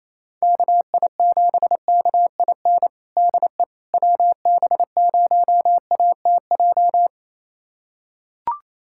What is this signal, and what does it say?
Morse code, heard as a keyed tone, about 28 wpm, 700 Hz. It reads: KI7KIN DE WB0ATJ
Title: USA Callsign DE Callsign 28wpm